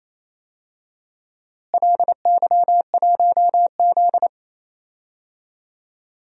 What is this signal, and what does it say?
Morse code, heard as a keyed tone, about 28 wpm, 700 Hz. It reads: LY1Z
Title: DXCC-100 Call Signs - 4 Characters - Repeated 3x - 28wpm